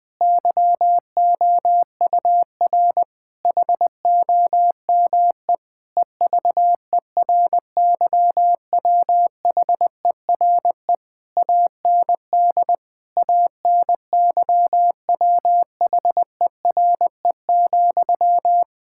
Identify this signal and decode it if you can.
Morse code, heard as a keyed tone, about 20 wpm, 700 Hz. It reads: YOUR HOME EVERYWHERE AND ANYWHERE,
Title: Sentences from Top 1600 Words 20wpm